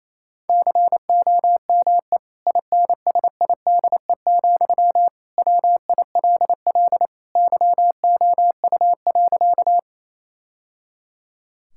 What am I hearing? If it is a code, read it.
Morse code, heard as a keyed tone, about 28 wpm, 700 Hz. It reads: COME INSIDE, WILL YOU.
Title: Sentences from Top 1000 Words - Review Part 6 - 28wpm